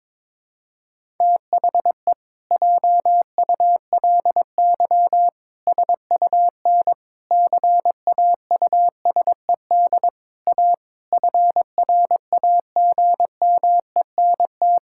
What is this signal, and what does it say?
Morse code, heard as a keyed tone, about 22 wpm, 700 Hz. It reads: THE JULY SUN CAUSED A FRAGMENT
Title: Pangram Challenge v2 - 22wpm